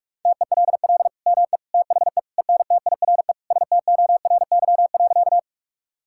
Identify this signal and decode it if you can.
Morse code, heard as a keyed tone, about 42 wpm, 700 Hz. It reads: TELL ME THE ENTIRE STORY.
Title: Sentences from Top 1000 Words - Review Part 5 - 45wpm